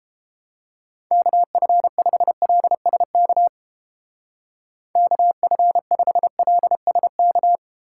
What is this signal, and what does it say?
Morse code, heard as a keyed tone, about 32 wpm, 700 Hz. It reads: KF5LSK KF5LSK